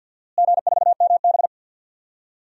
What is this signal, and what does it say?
Morse code, heard as a keyed tone, about 50 wpm, 700 Hz. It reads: G4MB
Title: DXCC-100 Call Signs - 4 Characters - Repeated 2x - 50wpm